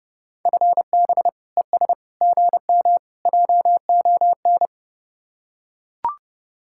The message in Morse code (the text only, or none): FB ES GM JON